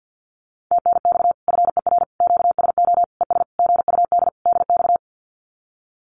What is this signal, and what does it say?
Morse code, heard as a keyed tone, about 50 wpm, 700 Hz. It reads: TNX FER QSO ES GUD DX
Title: Common QSO Phrases Encore 50wpm